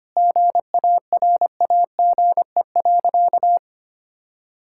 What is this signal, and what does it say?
Morse code, heard as a keyed tone, about 25 wpm, 700 Hz. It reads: GARAGE.
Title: Sentences from Top 1600 Words 25wpm